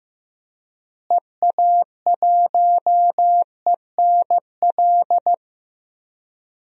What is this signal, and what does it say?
Morse code, heard as a keyed tone, about 14 wpm, 700 Hz. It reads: EA1ENL
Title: DXCC-100 Call Signs - 6 Characters - Repeated 3x - 15wpm